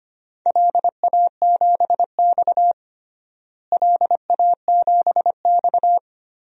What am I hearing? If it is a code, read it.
Morse code, heard as a keyed tone, about 25 wpm, 700 Hz. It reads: LA7X LA7X
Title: DXCC-100 Call Signs - 4 Characters - Repeated 3x - 25wpm